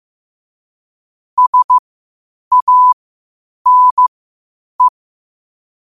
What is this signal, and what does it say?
Morse code, heard as a keyed tone, about 15 wpm, spaced as for 8 wpm, 1000 Hz. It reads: SANE
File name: qotdfiles / 8wpm